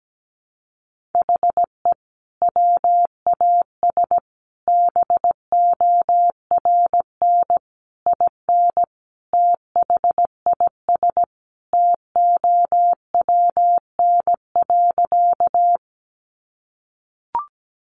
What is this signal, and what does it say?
Morse code, heard as a keyed tone, about 17 wpm, 700 Hz. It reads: HE WAS BORN IN THIS TOWN.
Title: Sentences from Top 1000 Words 17wpm